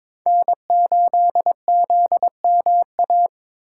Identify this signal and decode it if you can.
Morse code, heard as a keyed tone, about 26 wpm, 700 Hz. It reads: N8ZMA
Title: Speed Racing - USA 1x3 Call Signs 22wpm